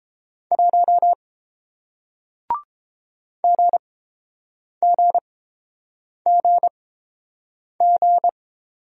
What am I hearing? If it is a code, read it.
Morse code, heard as a keyed tone, about 30 wpm, spaced as for 7 wpm, 700 Hz. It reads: 1 GGGG